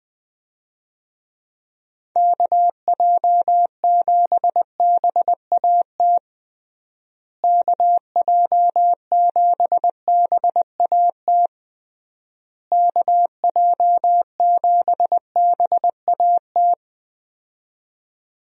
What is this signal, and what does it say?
Morse code, heard as a keyed tone, about 20 wpm, 700 Hz. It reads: KJ7BAT KJ7BAT KJ7BAT
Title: USA Call Signs - 6 Characters - Repeated 3x - 20wpm